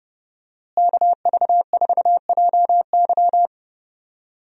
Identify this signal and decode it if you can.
Morse code, heard as a keyed tone, about 30 wpm, 700 Hz. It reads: KV4JY
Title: USA Call Signs - 5 Characters - Repeated 3x - 30wpm